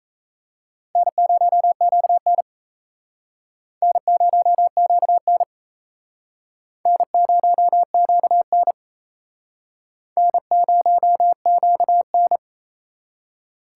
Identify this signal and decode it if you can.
Morse code, heard as a keyed tone, about 34 wpm, 700 Hz. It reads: N0QN N0QN N0QN N0QN
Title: Speed Racing - USA 1x2 Call Signs 28wpm